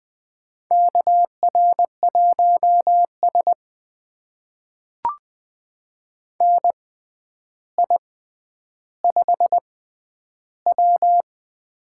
Morse code, heard as a keyed tone, about 20 wpm, spaced as for 10 wpm, 700 Hz. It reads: KR1S NI5W
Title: USA Call Signs - 4 Characters - 6x Character Spacing - 20wpm